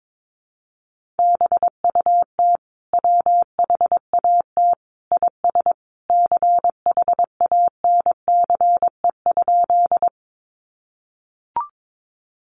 Morse code, heard as a keyed tone, about 22 wpm, 700 Hz. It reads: BUT WHAT IS CHANCE?
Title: Sentences from Top 800 Words 22wpm